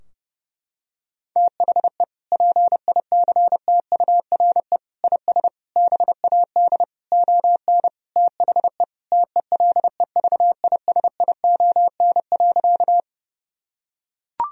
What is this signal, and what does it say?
Morse code, heard as a keyed tone, about 32 wpm, 700 Hz. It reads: THE PICTURE IS BAD ON THE TELEVISION.
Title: Sentences from Top 1000 Words - Review Part 4 - 30wpm